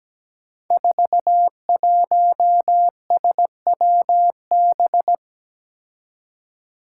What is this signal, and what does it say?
Morse code, heard as a keyed tone, about 17 wpm, 700 Hz. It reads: 41SWB
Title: Five Challenging Letter-Number Combinations 17wpm